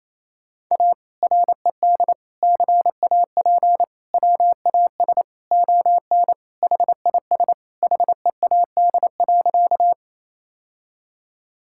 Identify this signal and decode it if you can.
Morse code, heard as a keyed tone, about 28 wpm, 700 Hz. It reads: A RED CAP WAS ON HIS HEAD.
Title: Sentences from Top 1600 Words 28wpm